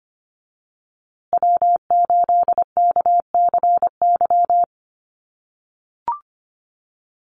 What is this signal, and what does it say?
Morse code, heard as a keyed tone, about 25 wpm, 700 Hz. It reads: W8KCY
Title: USA Call Signs Repeated 3x 25wpm